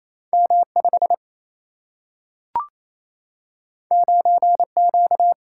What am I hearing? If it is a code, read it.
Morse code, heard as a keyed tone, about 28 wpm, 700 Hz. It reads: M5 9Q
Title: Single-Letter Single-Number 28wpm